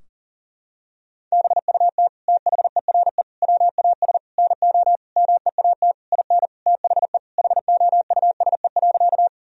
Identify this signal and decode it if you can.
Morse code, heard as a keyed tone, about 40 wpm, 700 Hz. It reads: BUT THERE WAS NO MEAT IN THE HOUSE.
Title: Sentences from Top 1000 Words - Review Part 3 - 40wpm